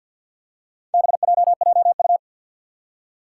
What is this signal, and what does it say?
Morse code, heard as a keyed tone, about 49 wpm, 700 Hz. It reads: BJJU
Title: Four Challenging Letter-Number Combinations 50wpm